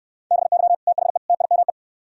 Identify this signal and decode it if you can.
Morse code, heard as a keyed tone, about 69 wpm, 700 Hz. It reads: BY THE TIME